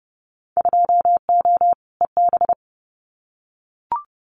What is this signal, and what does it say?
Morse code, heard as a keyed tone, about 30 wpm, 700 Hz. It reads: 2O EB
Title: Winter Field Day Exchanges Encore 30wpm